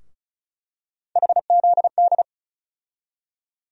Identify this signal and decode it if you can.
Morse code, heard as a keyed tone, about 36 wpm, 700 Hz. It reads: HZD
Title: USA Call Sign Postfixes - 3 Letters 35wpm